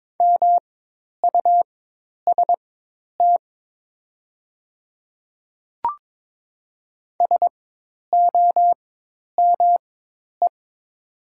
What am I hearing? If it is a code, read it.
Morse code, heard as a keyed tone, about 23 wpm, spaced as for 10 wpm, 700 Hz. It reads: MUST SOME
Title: 4x Character Spacing - Top 200 Words - 22wpm